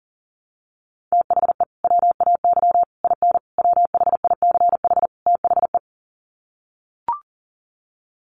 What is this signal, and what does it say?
Morse code, heard as a keyed tone, about 40 wpm, 700 Hz. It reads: THE WAY IN WHICH THE
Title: Sets of 5 Words Encore2 40wpm